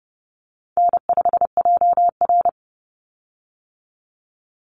Morse code, heard as a keyed tone, about 30 wpm, 700 Hz. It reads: N5JR
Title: Easy Call Signs Repeated 2x - 30wpm